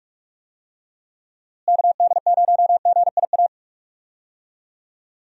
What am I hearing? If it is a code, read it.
Morse code, heard as a keyed tone, about 45 wpm, 700 Hz. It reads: KD0GIA
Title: USA Call Signs - 6 Characters - Repeated 3x - 45wpm